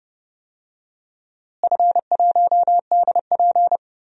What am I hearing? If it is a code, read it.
Morse code, heard as a keyed tone, about 30 wpm, 700 Hz. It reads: F1DP
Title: DXCC-100 Call Signs - 4 Characters - Repeated 2x - 30wpm